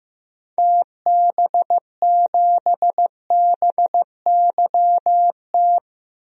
Morse code, heard as a keyed tone, about 16 wpm, 700 Hz. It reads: TB7BYT
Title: DXCC-100 Call Signs - 6 Characters - Repeated 3x - 15wpm